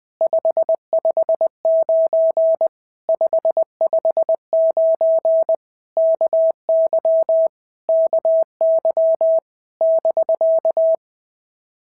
Morse code, heard as a keyed tone, about 20 wpm, 650 Hz. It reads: 559 559 KY KY <BK>
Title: POTA QSO Practice - 20wpm